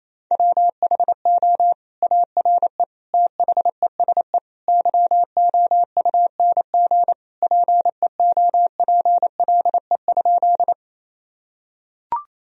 Morse code, heard as a keyed tone, about 28 wpm, 700 Hz. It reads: WHO ARE THESE YOUNG PEOPLE?
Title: Sentences from Top 1000 Words - Review Part 1 - 28wpm